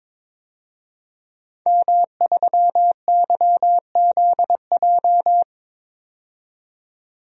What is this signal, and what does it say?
Morse code, heard as a keyed tone, about 22 wpm, 700 Hz. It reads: M3YZJ
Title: DXCC-100 Call Signs - 5 Characters - 22wpm